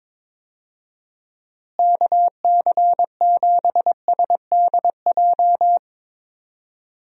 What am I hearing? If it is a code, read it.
Morse code, heard as a keyed tone, about 22 wpm, 700 Hz. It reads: KC7SDJ
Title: USA Call Signs - 6 Characters - Repeated 3x - 22wpm